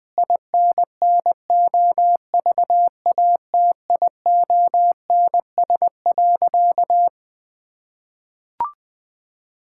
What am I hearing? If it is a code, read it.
Morse code, heard as a keyed tone, about 20 wpm, 700 Hz. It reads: INNOVATIONS.
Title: Sentences with 8-Word Phrases - 20wpm